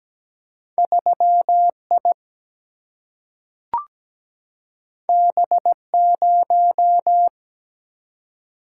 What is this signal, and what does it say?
Morse code, heard as a keyed tone, about 17 wpm, 700 Hz. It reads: 3I B0
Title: Single-Letter Single-Number 17wpm